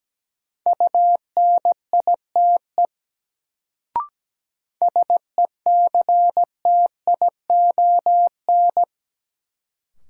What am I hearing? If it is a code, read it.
Morse code, heard as a keyed tone, about 17 wpm, 700 Hz. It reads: UNITE SECTION
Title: Sentences from Top 1000 Words - Review Part 5 - New Word Drill - 17wpm